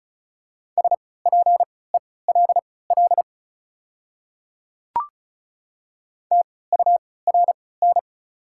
Morse code, heard as a keyed tone, about 34 wpm, spaced as for 20 wpm, 700 Hz. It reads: SPELL TURN
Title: 3x Character Spacing - Top 300 Words - 35wpm